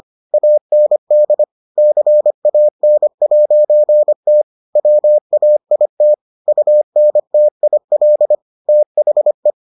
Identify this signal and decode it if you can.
Morse code, heard as a keyed tone, about 25 wpm, 600 Hz. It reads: AND CAN'T WAIT UNTIL THE